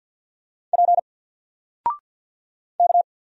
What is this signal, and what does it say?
Morse code, heard as a keyed tone, about 48 wpm, 700 Hz. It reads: P K
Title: Single Letter-Number 50wpm